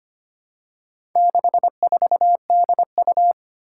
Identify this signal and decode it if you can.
Morse code, heard as a keyed tone, about 25 wpm, 700 Hz. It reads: 64DU
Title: Four Challenging Letter-Number Combinations 25wpm